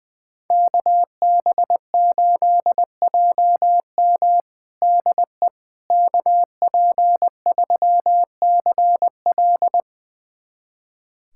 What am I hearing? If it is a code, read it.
Morse code, heard as a keyed tone, about 20 wpm, 700 Hz. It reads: KB8JM DE KP3CL
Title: USA Callsign DE Callsign - 5 Characters - 20wpm